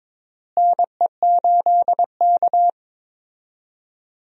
Morse code, heard as a keyed tone, about 22 wpm, 700 Hz. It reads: NE8K
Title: USA Call Signs - 4 Characters - Repeated 3x - 22wpm